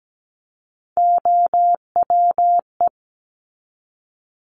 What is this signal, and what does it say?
Morse code, heard as a keyed tone, about 17 wpm, 700 Hz. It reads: OWE
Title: Top 1501-1750 Words 17wpm